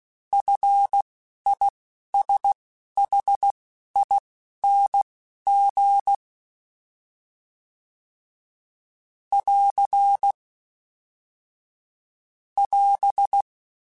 Morse code, heard as a keyed tone, about 16 wpm, spaced as for 11 wpm, 800 Hz. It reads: FISHING <AR> <AS>
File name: Short Runs / Short Runs-mp3s / SinglePrefixes @10